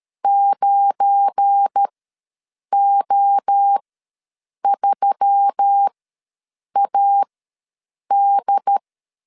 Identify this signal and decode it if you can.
Morse code, heard as a keyed tone, about 13 wpm, spaced as for 7 wpm, 800 Hz. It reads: 9O3AD
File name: LEZIONI / 8 YJ45 / 8 YJ45 ws 12wpm 3146